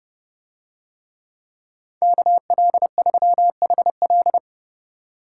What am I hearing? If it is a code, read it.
Morse code, heard as a keyed tone, about 29 wpm, 700 Hz. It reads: KL3HL